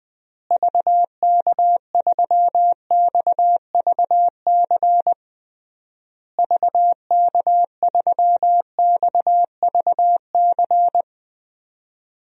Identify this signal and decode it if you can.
Morse code, heard as a keyed tone, about 20 wpm, 700 Hz. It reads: VK3XVC VK3XVC
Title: DXCC-100 Call Signs - 6 Characters - Repeated 3x - 20wpm